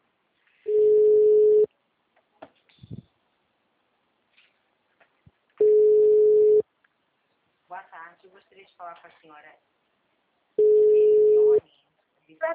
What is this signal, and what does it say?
Ringback tone (Europe, 425 Hz), with speech.